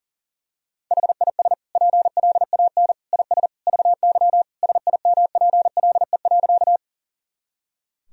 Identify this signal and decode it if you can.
Morse code, heard as a keyed tone, about 40 wpm, 700 Hz. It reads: HIS PLAN IS VY SIMPLE.
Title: Sentences from Top 500 Words and Abbreviations 40wpm